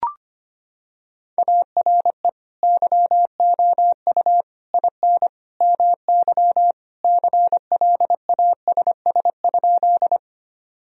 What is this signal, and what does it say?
Morse code, heard as a keyed tone, about 25 wpm, 700 Hz. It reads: ARE YOU IN MY CLASS?
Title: Sentences from Top 1000 Words - Review Part 2 - 25wpm